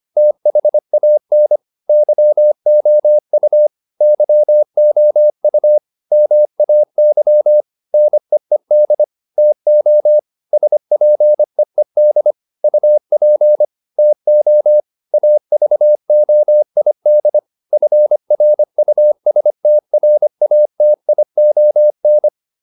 Morse code, heard as a keyed tone, about 25 wpm, 600 Hz. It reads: THAN YOU YOU MAY NEED TO SPEED UP TO AVOID FRUSTRATION